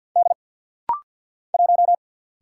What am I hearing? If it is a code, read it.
Morse code, heard as a keyed tone, about 50 wpm, 700 Hz. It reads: D 1
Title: Single Letter-Number - Rapid-Fire 50wpm